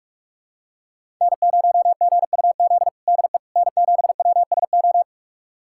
Morse code, heard as a keyed tone, about 46 wpm, 700 Hz. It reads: N0GUZ DE N7WSO
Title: USA Callsign DE Callsign - 5 Characters - 45wpm